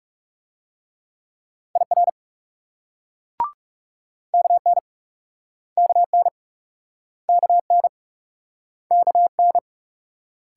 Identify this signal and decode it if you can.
Morse code, heard as a keyed tone, about 38 wpm, 700 Hz. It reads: IR KN KN KN KN
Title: Speed-Racing Multi-letter Phonemes - 30wpm